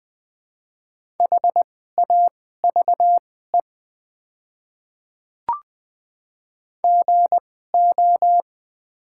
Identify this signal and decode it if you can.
Morse code, heard as a keyed tone, about 20 wpm, spaced as for 14 wpm, 700 Hz. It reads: HAVE GO
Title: 2x Character Spacing - Top 100 Words - 20wpm